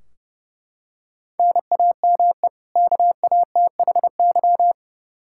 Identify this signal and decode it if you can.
Morse code, heard as a keyed tone, about 32 wpm, 700 Hz. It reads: NAME KATHY
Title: Practice QSO - Exchange 1 - Beginning - 30wpm